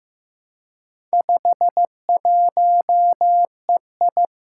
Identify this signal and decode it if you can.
Morse code, heard as a keyed tone, about 15 wpm, 700 Hz. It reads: 51EI
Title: Four Challenging Letter-Number Combinations 15wpm